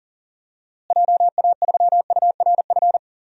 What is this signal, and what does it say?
Morse code, heard as a keyed tone, about 40 wpm, 700 Hz. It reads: JA3URF
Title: DXCC-100 Call Signs - 6 Characters - Repeated 2x - 40wpm